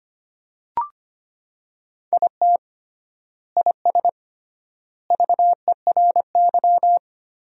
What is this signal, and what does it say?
Morse code, heard as a keyed tone, about 25 wpm, 700 Hz. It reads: IT IS VERY
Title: 3x Word-Spacing - Sentences from Top 500 Words - 25wpm